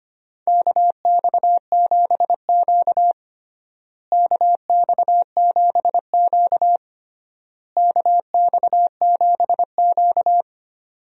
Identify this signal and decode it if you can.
Morse code, heard as a keyed tone, about 25 wpm, 700 Hz. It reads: KX7Q KX7Q KX7Q
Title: USA Call Signs - 4 Characters - Repeated 3x - 25wpm